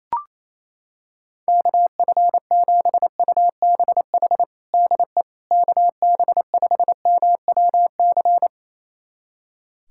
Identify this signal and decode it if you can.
Morse code, heard as a keyed tone, about 27 wpm, 700 Hz. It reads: KF7UBH DE KB5MWC
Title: USA Callsign DE Callsign - 6 Characters - 28wpm